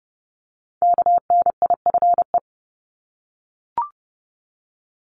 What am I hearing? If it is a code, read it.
Morse code, heard as a keyed tone, about 30 wpm, 700 Hz. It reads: KNIFE
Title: Multi-letter Phonemes 30wpm